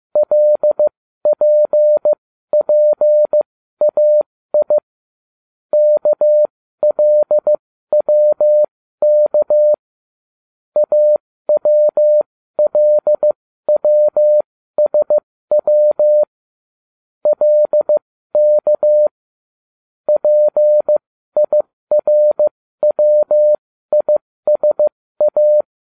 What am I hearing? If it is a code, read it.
Morse code, heard as a keyed tone, about 15 wpm, spaced as for 13 wpm, 600 Hz. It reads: LPPAI KLWK AWLWSW LK PIRWISA